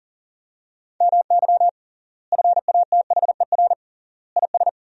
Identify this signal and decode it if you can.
Morse code, heard as a keyed tone, about 40 wpm, 700 Hz. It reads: MY FATHER IS
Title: 3x Word-Spacing - Sentences from Top 300 Words - 40wpm